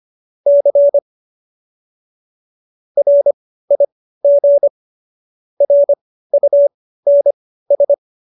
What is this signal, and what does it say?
Morse code, heard as a keyed tone, about 25 wpm, spaced as for 15 wpm, 570 Hz. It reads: C RIG RUNS